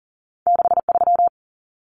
Morse code, heard as a keyed tone, about 40 wpm, 700 Hz. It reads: 63
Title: Common Cut Numbers - 2-Digit 40wpm